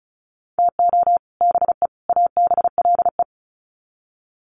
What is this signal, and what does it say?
Morse code, heard as a keyed tone, about 34 wpm, 700 Hz. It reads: TO BE ABLE
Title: Sets of 3 Words 35wpm